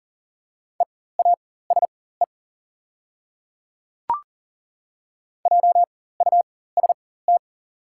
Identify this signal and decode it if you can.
Morse code, heard as a keyed tone, about 39 wpm, spaced as for 19 wpm, 700 Hz. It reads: EASE JUST
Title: 4x Character Spacing - Top 300 Words - 40wpm